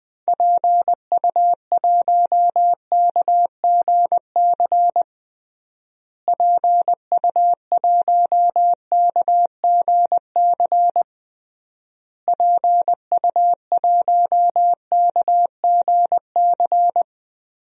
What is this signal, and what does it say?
Morse code, heard as a keyed tone, about 20 wpm, 700 Hz. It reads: PU1KGC PU1KGC PU1KGC
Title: DXCC-100 Call Signs - 6 Characters - Repeated 3x - 20wpm